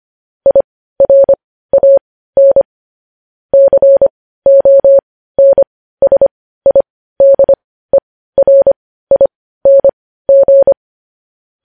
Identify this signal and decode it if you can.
Morse code, heard as a keyed tone, about 25 wpm, spaced as for 15 wpm, 550 Hz. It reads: IRAN CONSIDERING
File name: podcast / News headlines-1x-25x15